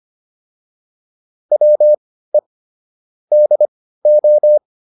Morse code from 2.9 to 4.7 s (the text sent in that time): DO